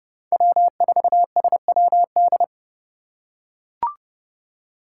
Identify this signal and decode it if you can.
Morse code, heard as a keyed tone, about 30 wpm, 700 Hz. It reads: W4SWD